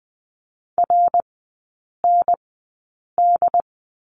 Morse code, heard as a keyed tone, about 20 wpm, spaced as for 8 wpm, 700 Hz. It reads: RND
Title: 067 Lesson 12e - Letter d - Three Characters 20wpm